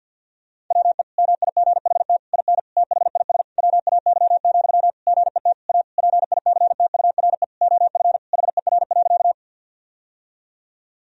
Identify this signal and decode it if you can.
Morse code, heard as a keyed tone, about 50 wpm, 700 Hz. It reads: WE MIGHT IN THIS WAY, GET A PICTURE OF HER.
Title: Sentences from Top 1000 Words - Review Part 1 - 50wpm